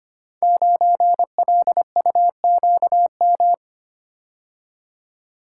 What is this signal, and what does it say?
Morse code, heard as a keyed tone, about 25 wpm, 700 Hz. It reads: 9LUQM
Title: Code Groups - 5 Letter:Numbers - 25wpm